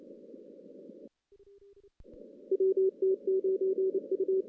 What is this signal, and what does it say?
Morse code, heard as a keyed tone, about 30 wpm, 380 Hz. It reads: WT9U
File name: 2016 JAN CW NAQP / 2016 Jan naqp k5ot - 0-10